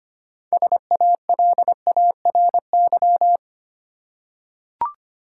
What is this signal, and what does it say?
Morse code, heard as a keyed tone, about 25 wpm, 700 Hz. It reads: SALARY
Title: Six Letter Words - 25wpm